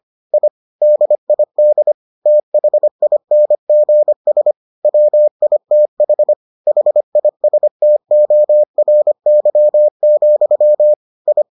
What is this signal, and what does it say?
Morse code, heard as a keyed tone, about 25 wpm, 600 Hz. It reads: I DID THINGS WITH HISTORY, I